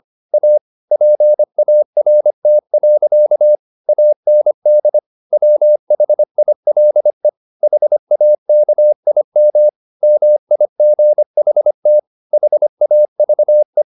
Morse code, heard as a keyed tone, about 25 wpm, 600 Hz. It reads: A PART. AND WHILE HAKIM MIGHT HAVE